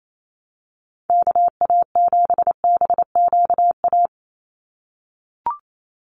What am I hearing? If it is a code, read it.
Morse code, heard as a keyed tone, about 28 wpm, 700 Hz. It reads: KA7BQA
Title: USA Call Signs 28wpm